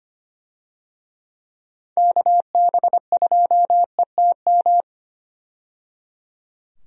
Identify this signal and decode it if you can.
Morse code, heard as a keyed tone, about 25 wpm, 700 Hz. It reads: KB2ETM
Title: USA Call Signs - 6 Characters - 25wpm